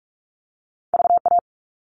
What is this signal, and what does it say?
Morse code, heard as a keyed tone, about 46 wpm, 700 Hz. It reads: VA
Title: ARRL Sections 45wpm